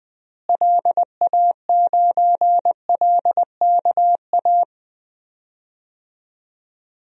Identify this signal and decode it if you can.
Morse code, heard as a keyed tone, about 20 wpm, 700 Hz. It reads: LA9LKA
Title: DXCC-100 Call Signs - 6 Characters - Repeated 2x - 20wpm